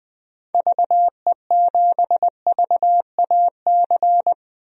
Morse code, heard as a keyed tone, about 20 wpm, 700 Hz. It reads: VE7VAC
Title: DXCC-100 Call Signs - 6 Characters - 20wpm